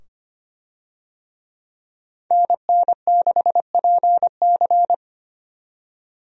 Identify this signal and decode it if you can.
Morse code, heard as a keyed tone, about 25 wpm, 700 Hz. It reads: NN6PC
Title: USA Call Signs - 5 Characters - 25wpm